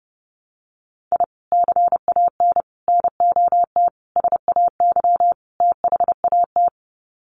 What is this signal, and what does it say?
Morse code, heard as a keyed tone, about 30 wpm, 700 Hz. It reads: I CAN NOT SAY THAT
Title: Sets of 5 Words Encore2 30wpm